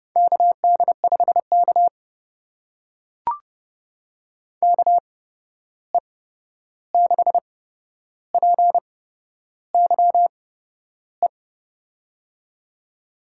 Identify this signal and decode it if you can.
Morse code, heard as a keyed tone, about 30 wpm, spaced as for 11 wpm, 700 Hz. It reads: KD5K KE6PYE
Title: USA Call Signs - 8x Character Spacing - 30wpm